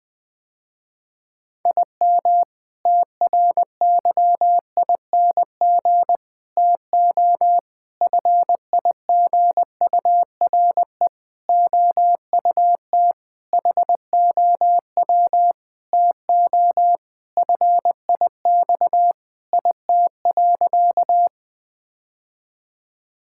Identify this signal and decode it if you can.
Morse code, heard as a keyed tone, about 20 wpm, 700 Hz. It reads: IM TRYING TO FIGURE OUT HOW TO FIX IT.
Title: Sentences with 6-Word Phrases - Spoken Beginning - 20wpm